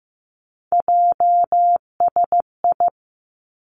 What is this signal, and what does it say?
Morse code, heard as a keyed tone, about 15 wpm, 700 Hz. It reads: JSI
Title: Three Challenging Letter-Number Combinations 15wpm